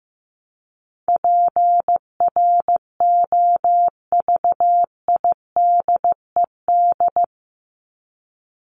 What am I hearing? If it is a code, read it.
Morse code, heard as a keyed tone, about 15 wpm, 700 Hz. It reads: PROVIDED
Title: Top 1751-2000 Words 15wpm